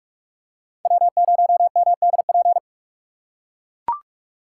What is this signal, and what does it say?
Morse code, heard as a keyed tone, about 43 wpm, 700 Hz. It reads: W0MDP
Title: USA Call Signs - 4x Character Spacing - 45wpm